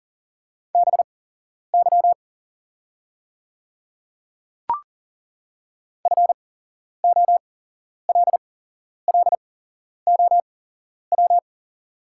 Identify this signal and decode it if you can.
Morse code, heard as a keyed tone, about 40 wpm, spaced as for 11 wpm, 690 Hz. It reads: BY FOLLOW
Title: 8x Character Spacing - Top 200 Words - 40wpm